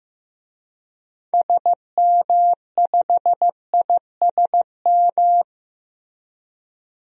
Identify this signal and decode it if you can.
Morse code, heard as a keyed tone, about 15 wpm, 700 Hz. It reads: SM5ISM
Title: DXCC-100 Call Signs - 6 Characters - Repeated 3x - 15wpm